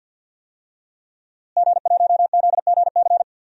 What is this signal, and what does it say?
Morse code, heard as a keyed tone, about 52 wpm, 700 Hz. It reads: G1ZGC
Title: DXCC-100 Call Signs - 5 Characters - Repeated 3x - 50wpm